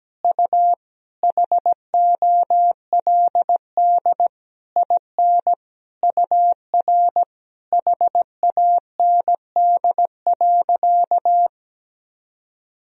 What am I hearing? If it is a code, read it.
Morse code, heard as a keyed tone, about 17 wpm, 700 Hz. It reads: U HOLD IN UR HAND.
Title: Sentences from Top 400 Words and Abbreviations 17wpm